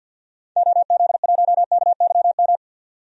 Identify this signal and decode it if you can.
Morse code, heard as a keyed tone, about 52 wpm, 700 Hz. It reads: OZ1KYM
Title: DXCC-100 Call Signs - 6 Characters - 50wpm